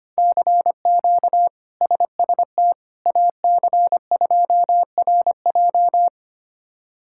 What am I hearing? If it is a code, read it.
Morse code, heard as a keyed tone, about 25 wpm, 700 Hz. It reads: CQ SST AC2RJ
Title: K1USN SST QSO Practice - 25wpm